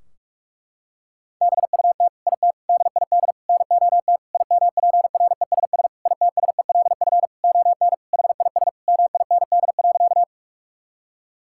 Morse code, heard as a keyed tone, about 46 wpm, 700 Hz. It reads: BUT IT DID NOT IMPRESS ITSELF ON HIS MIND.
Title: Sentences from Top 1600 Words 45wpm